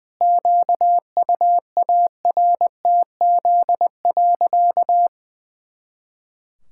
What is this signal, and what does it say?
Morse code, heard as a keyed tone, about 20 wpm, 700 Hz. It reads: QUARTZ.
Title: Pangram Challenge v2 - 20wpm